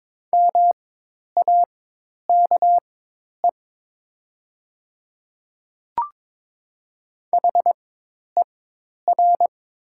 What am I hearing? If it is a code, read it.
Morse code, heard as a keyed tone, about 22 wpm, spaced as for 10 wpm, 700 Hz. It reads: MAKE HER
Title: 4x Character Spacing - Top 100 Words - 22wpm